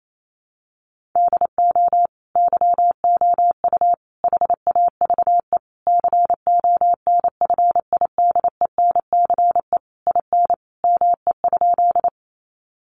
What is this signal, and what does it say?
Morse code, heard as a keyed tone, about 28 wpm, 700 Hz. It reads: DO YOU HAVE CONFIDENCE IN ME?
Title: Sentences from Top 1500 Words 28wpm